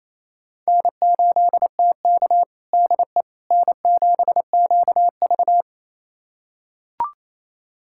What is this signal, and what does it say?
Morse code, heard as a keyed tone, about 27 wpm, 700 Hz. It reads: N8TK DE N7QV
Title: USA Callsign DE Callsign - 4 Characters - 28wpm